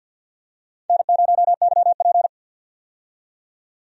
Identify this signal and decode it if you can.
Morse code, heard as a keyed tone, about 51 wpm, 700 Hz. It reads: N0YP
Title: USA Call Signs - 4 Characters - 50wpm